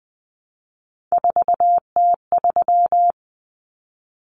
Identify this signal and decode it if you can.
Morse code, heard as a keyed tone, about 20 wpm, 700 Hz. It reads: 4T3
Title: Common Cut Numbers - 3-Digit 20wpm